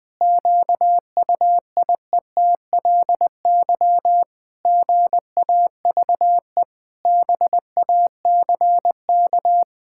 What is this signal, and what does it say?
Morse code, heard as a keyed tone, about 20 wpm, 700 Hz. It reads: QUIETLY GAVE BACK
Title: Pangram Challenge v2 - 20wpm